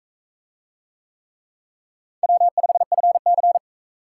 Morse code, heard as a keyed tone, about 39 wpm, 700 Hz. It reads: W5FC